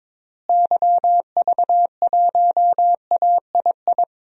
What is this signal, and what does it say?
Morse code, heard as a keyed tone, about 21 wpm, 700 Hz. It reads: YV1AII
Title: DXCC-100 Call Signs - 6 Characters - Repeated 2x - 22wpm